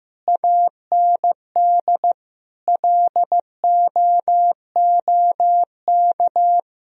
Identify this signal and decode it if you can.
Morse code, heard as a keyed tone, about 15 wpm, 700 Hz. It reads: AND LOOK
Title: Sentences from Top 1000 Words - Review Part 2 - 15wpm